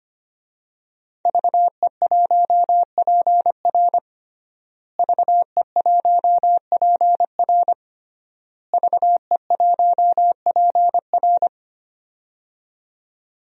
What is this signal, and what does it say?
Morse code, heard as a keyed tone, about 25 wpm, 700 Hz. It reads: VE1PR VE1PR VE1PR
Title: DXCC-100 Call Signs - 5 Characters - Repeated 3x - 25wpm